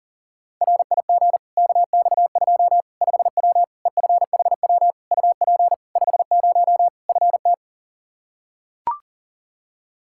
Morse code, heard as a keyed tone, about 40 wpm, 700 Hz. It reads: RIG KX2 5W EFHW UP 50 FT
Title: Practice QSO - Exchange 2 - Info - 40wpm